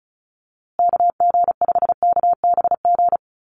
Morse code, heard as a keyed tone, about 34 wpm, 700 Hz. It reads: KG5KBG
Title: USA Call Signs Repeated 3x 35wpm